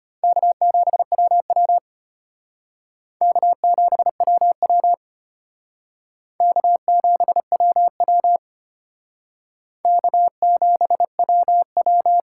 Morse code, heard as a keyed tone, about 32 wpm, 700 Hz. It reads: K7WW K7WW K7WW K7WW